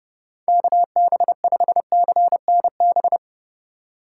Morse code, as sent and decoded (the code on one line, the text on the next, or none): -.- -... ..... -.-. -. -...
KB5CNB